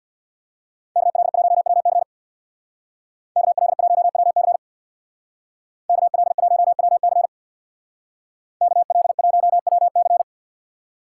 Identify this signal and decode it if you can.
Morse code, heard as a keyed tone, about 64 wpm, 700 Hz. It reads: KL1WC KL1WC KL1WC KL1WC